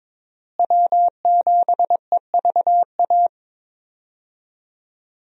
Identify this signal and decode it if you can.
Morse code, heard as a keyed tone, about 22 wpm, 700 Hz. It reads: W7EVA
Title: USA Call Signs - 5 Characters - Repeated 3x - 22wpm